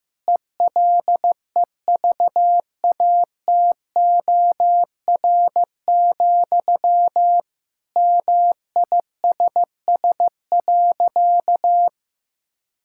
Morse code, heard as a keyed tone, about 15 wpm, 700 Hz. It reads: ELEVATOR, MISS.
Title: Sentences from Top 1600 Words 15wpm